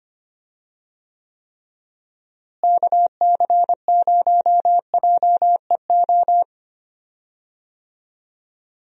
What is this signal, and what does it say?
Morse code, heard as a keyed tone, about 25 wpm, 700 Hz. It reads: KC0JEO
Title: USA Call Signs - 6 Characters - Repeated 3x - 25wpm